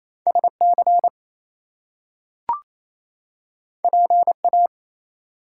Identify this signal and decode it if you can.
Morse code, heard as a keyed tone, about 28 wpm, 700 Hz. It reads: SC PA
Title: US States and Canadian Provinces 28wpm